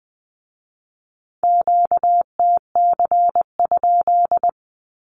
Morse code, heard as a keyed tone, about 19 wpm, 700 Hz. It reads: QTC?
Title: Q-Codes with Spoken Meaning - 20wpm